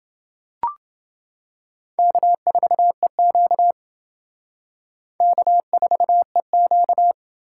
Morse code, heard as a keyed tone, about 28 wpm, 700 Hz. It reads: K4EQ K4EQ